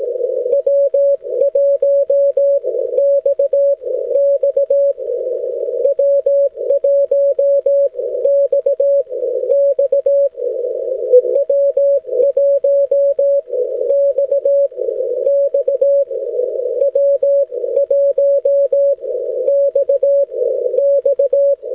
Morse code, heard as a keyed tone, about 18 wpm, spaced as for 13 wpm, 560 Hz. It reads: W1 XX W1 XX W1 XX W1 XX
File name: wl7e-echo-160m